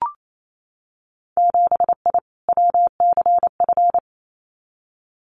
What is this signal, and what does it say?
Morse code, heard as a keyed tone, about 28 wpm, 700 Hz. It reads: 7I WCF
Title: Winter Field Day Exchanges Encore 28wpm